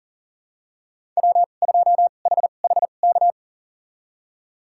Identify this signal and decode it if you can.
Morse code, heard as a keyed tone, about 40 wpm, spaced as for 29 wpm, 700 Hz. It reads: W2HHK
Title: USA Call Signs - 5 Characters - 2x Character Spacing - 40wpm